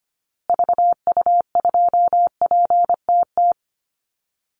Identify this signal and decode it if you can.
Morse code, heard as a keyed tone, about 25 wpm, 700 Hz. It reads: VU2PTT
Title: Call Signs Repeated 3x - 25wpm